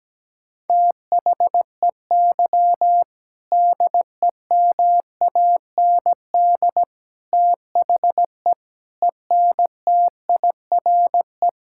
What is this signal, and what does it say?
Morse code, heard as a keyed tone, about 17 wpm, 700 Hz. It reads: THEY DEMAND THE ENTIRE
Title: Sentences from Top 1000 Words - Review Part 5 - 17wpm